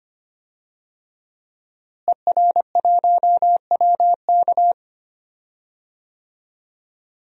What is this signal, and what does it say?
Morse code, heard as a keyed tone, about 25 wpm, 700 Hz. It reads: ER1WK
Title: DXCC-100 Call Signs - 5 Characters - Repeated 3x - 25wpm